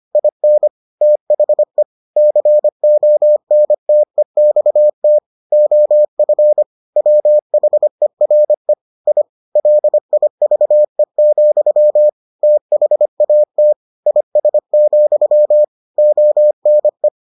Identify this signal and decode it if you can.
Morse code, heard as a keyed tone, about 25 wpm, 600 Hz. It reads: IN THE CONTEXT OF WHERE I LIVE, THAT IS, ONE